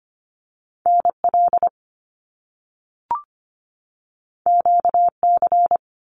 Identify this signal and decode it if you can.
Morse code, heard as a keyed tone, about 25 wpm, 700 Hz. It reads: NL QC
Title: Canadian Provinces and Territories 25wpm